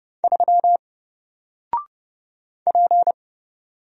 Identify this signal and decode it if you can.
Morse code, heard as a keyed tone, about 30 wpm, 700 Hz. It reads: 3 P
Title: Single Character (Letter:Number:Punctuation:Prosign) 30wpm